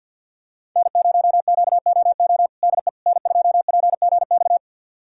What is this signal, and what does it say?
Morse code, heard as a keyed tone, about 46 wpm, 700 Hz. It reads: N0QOO DE N2PGX
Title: USA Callsign DE Callsign - 5 Characters - 50wpm